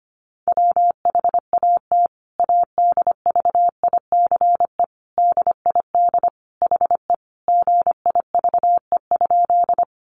Morse code, heard as a keyed tone, about 25 wpm, 700 Hz. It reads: WHAT ADVICE DID HE GIVE?
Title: Sentences from Top 1100 Words 25wpm